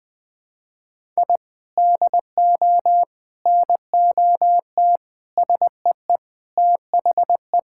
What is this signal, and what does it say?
Morse code, heard as a keyed tone, about 20 wpm, 700 Hz. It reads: I DO NOT SEE THE
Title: Sentences with 4-Word Phrases - Spoken Beginning - 20wpm